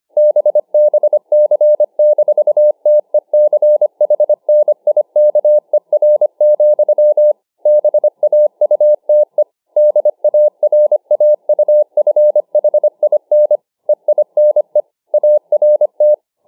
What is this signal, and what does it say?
Morse code, heard as a keyed tone, about 25 wpm, 600 Hz. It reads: BBC-TECHNIKER, BAUTE DARAUFHIN EINE ART